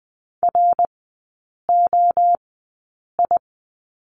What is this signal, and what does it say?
Morse code, heard as a keyed tone, about 20 wpm, spaced as for 8 wpm, 700 Hz. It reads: ROI
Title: 081 Lesson 14e - Number 2 - Three Characters 20wpm